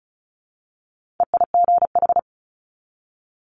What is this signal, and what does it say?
Morse code, heard as a keyed tone, about 33 wpm, 700 Hz. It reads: EIGH
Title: Multi-letter Phonemes 35wpm